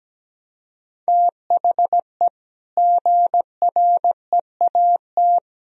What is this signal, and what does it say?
Morse code, heard as a keyed tone, about 17 wpm, 700 Hz. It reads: THE GREAT
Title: Sentences from Top 1000 Words - Review Part 4 - 17wpm